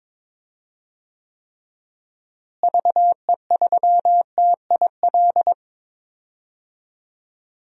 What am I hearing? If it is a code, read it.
Morse code, heard as a keyed tone, about 22 wpm, 700 Hz. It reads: VE3TIL
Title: DXCC-100 Call Signs - 6 Characters - Repeated 3x - 22wpm